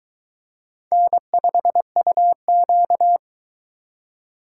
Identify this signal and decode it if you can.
Morse code, heard as a keyed tone, about 22 wpm, 700 Hz. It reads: N5UQ